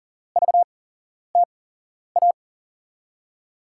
Morse code, heard as a keyed tone, about 39 wpm, spaced as for 11 wpm, 700 Hz. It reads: VTA
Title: USA Call Signs - 6 Characters - 8x Character Spacing - 40wpm